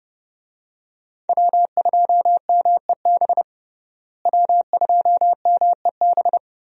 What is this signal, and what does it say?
Morse code, heard as a keyed tone, about 30 wpm, 700 Hz. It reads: W2MEB W2MEB
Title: USA Call Signs - 5 Characters - Repeated 2x - 30wpm